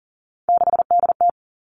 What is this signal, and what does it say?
Morse code, heard as a keyed tone, about 40 wpm, 700 Hz. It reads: 6DT
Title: All Cut Numbers - 3-Digit 40wpm